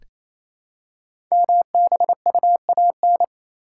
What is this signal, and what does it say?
Morse code, heard as a keyed tone, about 29 wpm, 700 Hz. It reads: MBUAN
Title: Code Groups - 5 Letters - 28wpm